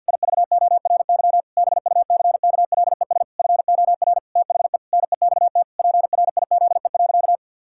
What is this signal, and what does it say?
Morse code, heard as a keyed tone, about 50 wpm, 700 Hz. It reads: IVORY BUCKLES FOR THE NEXT PRIZE.
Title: Pangram Challenge v2 - 50wpm